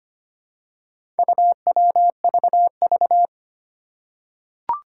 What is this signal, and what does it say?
Morse code, heard as a keyed tone, about 25 wpm, 700 Hz. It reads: UWVV
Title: Four Challenging Letter-Number Combinations 25wpm